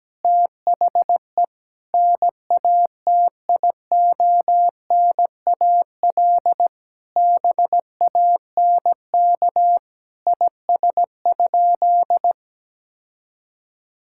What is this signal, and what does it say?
Morse code, heard as a keyed tone, about 17 wpm, 700 Hz. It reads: THE NATIONAL BANK IS?
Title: Sentences from Top 1000 Words - Review Part 6 - 17wpm